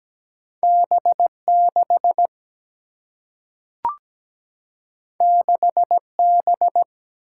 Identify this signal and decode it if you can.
Morse code, heard as a keyed tone, about 17 wpm, 700 Hz. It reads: B 6 6 B
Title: Single-Letter Single-Number 17wpm